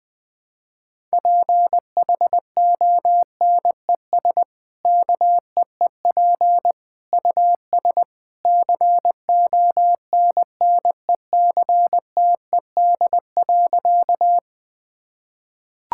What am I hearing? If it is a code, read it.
Morse code, heard as a keyed tone, about 20 wpm, 700 Hz. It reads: PHONES KEEP US CONNECTED.
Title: Easy Sentences with Hints - 20wpm